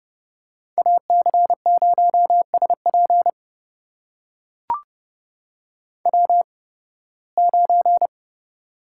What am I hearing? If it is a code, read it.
Morse code, heard as a keyed tone, about 30 wpm, spaced as for 20 wpm, 700 Hz. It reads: AC0SP W9
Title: USA Call Signs - 5 Characters - 8x Character Spacing - 30wpm